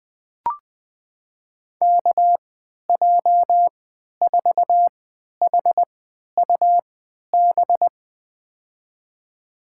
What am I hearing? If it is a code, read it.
Morse code, heard as a keyed tone, about 20 wpm, spaced as for 11 wpm, 700 Hz. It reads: KJ4HUB
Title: USA Call Signs - 6 Characters - 3x Character Spacing - 20wpm